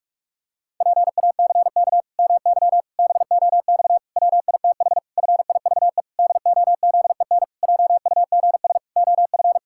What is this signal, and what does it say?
Morse code, heard as a keyed tone, about 46 wpm, 700 Hz. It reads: PACK MY BOX WITH FIVE DOZEN JUGS OF
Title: Pangram Challenge v2 - 45wpm